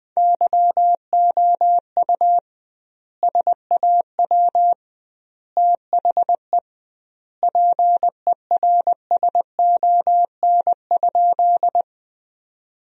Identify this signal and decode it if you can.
Morse code, heard as a keyed tone, about 21 wpm, 700 Hz. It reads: YOU SAW THE PERSON?
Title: 2x Word-Spacing - Sentences from Top 400 Words - 20wpm